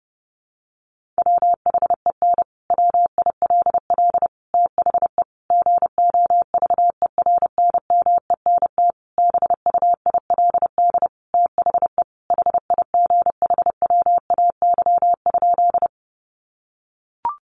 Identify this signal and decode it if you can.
Morse code, heard as a keyed tone, about 30 wpm, 700 Hz. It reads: WHEN WILL THE GOVERNMENT BUILD THE HIGHWAY?
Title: Sentences from Top 1400 Words 30wpm